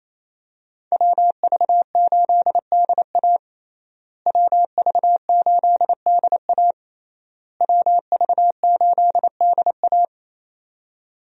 Morse code, heard as a keyed tone, about 27 wpm, 700 Hz. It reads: WV8DA WV8DA WV8DA
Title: USA Call Signs - 5 Characters - Repeated 3x - 28wpm